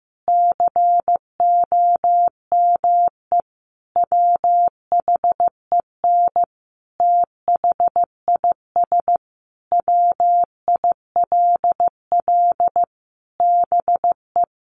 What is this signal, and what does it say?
Morse code, heard as a keyed tone, about 15 wpm, 700 Hz. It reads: COME WHEN THIS WILL BE
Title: Sentences from Top 1500 Words 15wpm